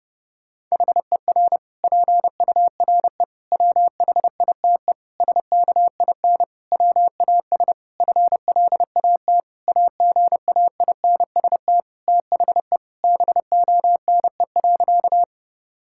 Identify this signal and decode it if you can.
Morse code, heard as a keyed tone, about 30 wpm, 700 Hz. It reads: HER PURE WHITE SKIN WAS FLAT AGAINST THE BONE.
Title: Sentences from Top 1600 Words 30wpm